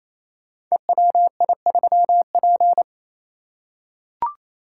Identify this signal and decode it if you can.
Morse code, heard as a keyed tone, about 27 wpm, 700 Hz. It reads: EWI3P
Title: Code Groups - 5 Letter:Numbers - 28wpm